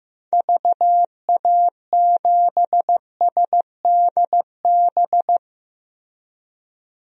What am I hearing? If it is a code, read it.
Morse code, heard as a keyed tone, about 16 wpm, 700 Hz. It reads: VA7SDB
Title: DXCC-100 Call Signs - 6 Characters - Repeated 2x - 15wpm